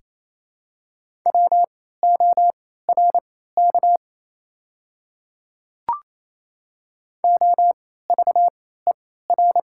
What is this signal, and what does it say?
Morse code, heard as a keyed tone, about 28 wpm, spaced as for 16 wpm, 700 Hz. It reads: WORK OVER
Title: 3x Character Spacing - Top 200 Words - 28wpm